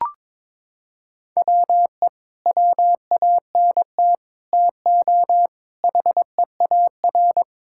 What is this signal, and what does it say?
Morse code, heard as a keyed tone, about 22 wpm, 700 Hz. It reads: WE WANT TO HEAR
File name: practice / 22wpm / Sentences with 8-Word Phrases - 22wpm